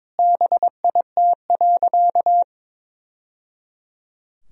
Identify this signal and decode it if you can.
Morse code, heard as a keyed tone, about 22 wpm, 700 Hz. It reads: BIT.
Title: Sentences from Top 1000 Words - Review Part 1 - 22wpm